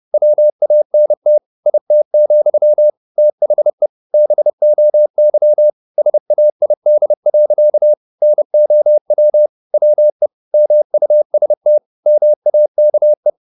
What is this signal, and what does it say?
Morse code, heard as a keyed tone, about 30 wpm, 600 Hz. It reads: WANT IT, THE BOY SAID. NOW WE MUST MAKE